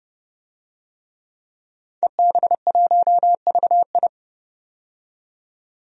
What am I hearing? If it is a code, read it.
Morse code, heard as a keyed tone, about 30 wpm, 700 Hz. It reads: EB1VI
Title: DXCC-100 Call Signs - 5 Characters - 30wpm